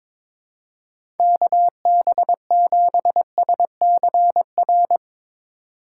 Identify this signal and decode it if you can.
Morse code, heard as a keyed tone, about 21 wpm, 700 Hz. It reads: KB7SCR
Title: USA Call Signs - 6 Characters - 3x Character Spacing - 22wpm